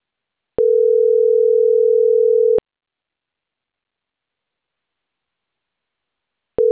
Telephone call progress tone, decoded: ringback tone

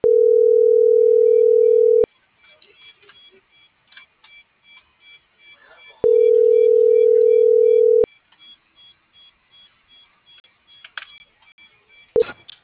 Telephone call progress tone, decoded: ringback tone

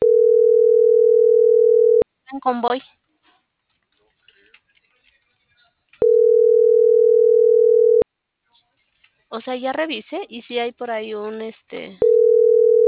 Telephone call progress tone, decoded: ringback tone

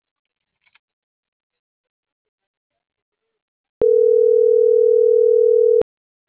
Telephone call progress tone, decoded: ringback tone

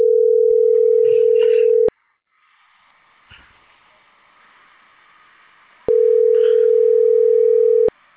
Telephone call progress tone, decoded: ringback tone

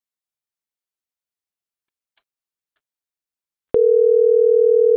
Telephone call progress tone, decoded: ringback tone

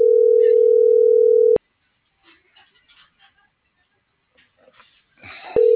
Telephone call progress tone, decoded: ringback tone

